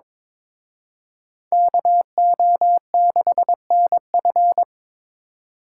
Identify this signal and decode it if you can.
Morse code, heard as a keyed tone, about 26 wpm, 700 Hz. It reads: KO6NF
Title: Speed Racing - USA 2x2 Call Signs 22wpm